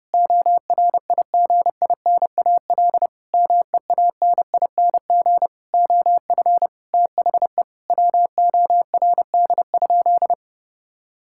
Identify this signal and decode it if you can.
Morse code, heard as a keyed tone, about 32 wpm, 700 Hz. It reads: ORIGINAL MEANING OF THE WORD?
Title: Sentences from Top 1000 Words - Review Part 6 - 30wpm